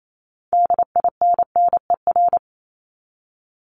Morse code, heard as a keyed tone, about 28 wpm, 700 Hz. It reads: DINNER
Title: Top 1751-2000 Words 28wpm